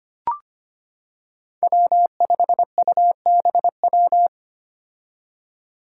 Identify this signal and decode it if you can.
Morse code, heard as a keyed tone, about 25 wpm, 700 Hz. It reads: W5UBW
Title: Five Challenging Letter-Number Combinations 25wpm